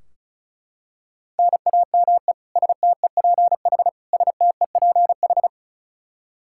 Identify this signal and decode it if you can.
Morse code, heard as a keyed tone, about 36 wpm, 700 Hz. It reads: NAME STEPH STEPH
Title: Practice QSO - Exchange 1 - Beginning - 35wpm